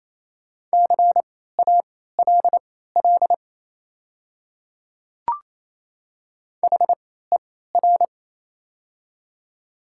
Morse code, heard as a keyed tone, about 28 wpm, spaced as for 16 wpm, 700 Hz. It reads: CALL HER
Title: 3x Character Spacing - Top 100 Words - 28wpm